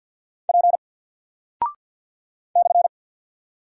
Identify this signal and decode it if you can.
Morse code, heard as a keyed tone, about 49 wpm, 700 Hz. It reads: P /
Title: Single Character (Letter:Number:Punctuation:Prosign) Alt 50wpm